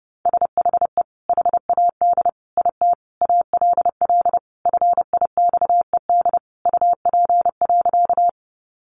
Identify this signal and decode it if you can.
Morse code, heard as a keyed tone, about 28 wpm, 700 Hz. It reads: SHE HAD IT ALL FIXED UP.
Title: Sentences from Top 1100 Words 30wpm